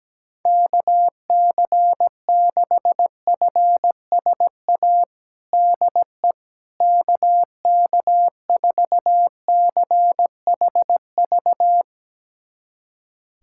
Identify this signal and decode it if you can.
Morse code, heard as a keyed tone, about 17 wpm, 700 Hz. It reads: KC6FSA DE KK4CHV
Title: USA Callsign DE Callsign 17wpm